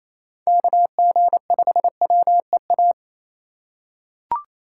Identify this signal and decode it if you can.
Morse code, heard as a keyed tone, about 28 wpm, 700 Hz. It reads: KG5WEA
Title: USA Call Signs - 6 Characters - 3x Character Spacing - 28wpm